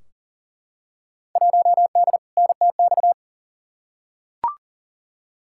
Morse code, heard as a keyed tone, about 40 wpm, 700 Hz. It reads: 1D NTX
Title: Field Day Exchanges - 40wpm